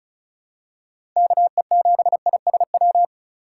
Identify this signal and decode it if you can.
Morse code, heard as a keyed tone, about 36 wpm, 700 Hz. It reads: KE7ISW
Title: USA Call Signs - 3x Character Spacing - 35wpm